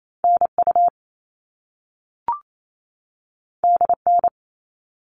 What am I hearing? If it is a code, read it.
Morse code, heard as a keyed tone, about 28 wpm, 700 Hz. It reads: NU DN
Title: All Cut Numbers - 2-Digit 28wpm